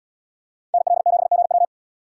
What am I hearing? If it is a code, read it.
Morse code, heard as a keyed tone, about 77 wpm, 700 Hz. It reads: NL7GW